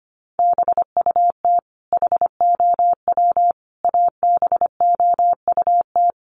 Morse code, heard as a keyed tone, about 25 wpm, 700 Hz. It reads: BUT HOW ABOUT
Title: Sentences from Top 100 Words 25wpm